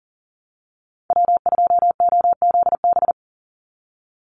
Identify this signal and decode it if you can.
Morse code, heard as a keyed tone, about 40 wpm, 700 Hz. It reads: W2OZB
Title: Call Signs Repeated 3x - 40wpm